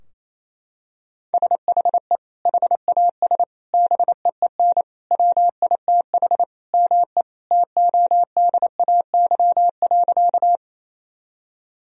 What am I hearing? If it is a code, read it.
Morse code, heard as a keyed tone, about 28 wpm, 700 Hz. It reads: SHE HAS BEEN WITH ME TODAY.
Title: Sentences from Top 1000 Words - Review Part 4 - 28wpm